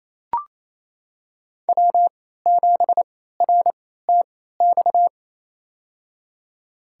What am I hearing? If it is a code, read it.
Morse code, heard as a keyed tone, about 28 wpm, spaced as for 16 wpm, 700 Hz. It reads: W7RTX
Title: USA Call Signs - 3x Character Spacing - 28wpm